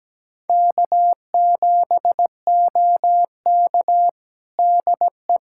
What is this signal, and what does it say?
Morse code, heard as a keyed tone, about 17 wpm, 700 Hz. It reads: K7OK DE
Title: USA Callsign DE Callsign - 4 Characters - 17wpm